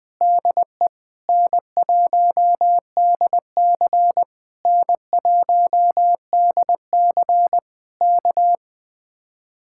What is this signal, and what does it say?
Morse code, heard as a keyed tone, about 20 wpm, 700 Hz. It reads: DE N1DC N1DC K
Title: Speed-Racing Common QSO Phrases Encore - 20wpm